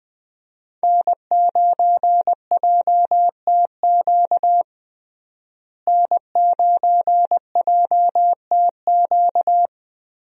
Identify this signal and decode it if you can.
Morse code, heard as a keyed tone, about 20 wpm, 700 Hz. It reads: N9JTQ N9JTQ
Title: USA Call Signs - 5 Characters - Repeated 3x - 20wpm